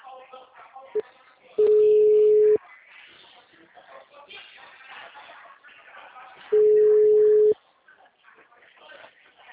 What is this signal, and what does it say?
Ringback tone (Europe, 425 Hz), with speech.